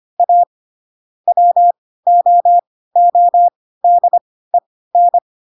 Morse code, heard as a keyed tone, about 25 wpm, spaced as for 16 wpm, 700 Hz. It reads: A WOODEN
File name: Pinocchio / Pinocchio ch7 10 25 ik1hge